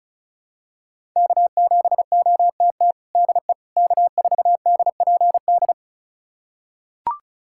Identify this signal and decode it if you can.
Morse code, heard as a keyed tone, about 36 wpm, 700 Hz. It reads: K7OTT DE K4DPD
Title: USA Callsign DE Callsign - 5 Characters - 35wpm